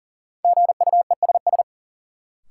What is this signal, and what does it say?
Morse code, heard as a keyed tone, about 40 wpm, 700 Hz. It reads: GUESS
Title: Sentences from Top 1500 Words - New Word Drill - 40wpm